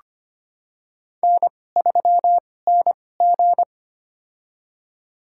Morse code, heard as a keyed tone, about 25 wpm, spaced as for 18 wpm, 700 Hz. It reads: N3NG
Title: USA Call Signs - 4 Characters - 2x Character Spacing - 25wpm